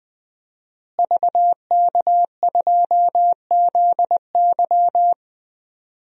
Morse code, heard as a keyed tone, about 20 wpm, 700 Hz. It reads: VK2ZY
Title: DXCC-100 Call Signs - 5 Characters - Repeated 2x - 20wpm